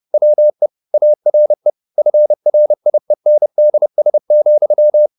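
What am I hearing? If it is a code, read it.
Morse code, heard as a keyed tone, about 30 wpm, 600 Hz. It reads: WE ARE FRIENDS,
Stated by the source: Morse code at 30 wpm, the novel Never by Ken Follett in English, from CW med Gnister.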